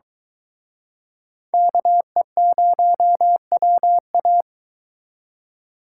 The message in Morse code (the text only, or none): KE0WA